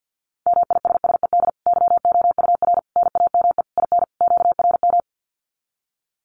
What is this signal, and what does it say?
Morse code, heard as a keyed tone, about 50 wpm, 700 Hz. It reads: MISSED YOUR NAME IN QRM
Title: Common QSO Phrases 50wpm